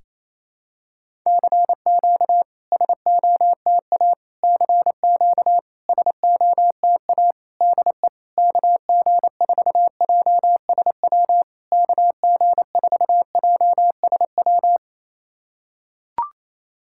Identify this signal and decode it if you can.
Morse code, heard as a keyed tone, about 28 wpm, 700 Hz. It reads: CQ SOTA CQ SOTA DE KG4JSW KG4JSW
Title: SOTA QSO Practice 28wpm